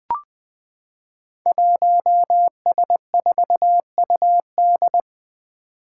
Morse code, heard as a keyed tone, about 20 wpm, 700 Hz. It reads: 1S4UD
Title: Five Challenging Letter-Number Combinations 20wpm